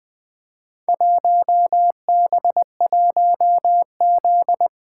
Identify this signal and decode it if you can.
Morse code, heard as a keyed tone, about 20 wpm, 700 Hz. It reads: 1B1Z
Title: Four Letter-Number Combinations - 20wpm